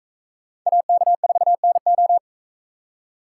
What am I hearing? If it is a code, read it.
Morse code, heard as a keyed tone, about 42 wpm, 700 Hz. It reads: AK4NO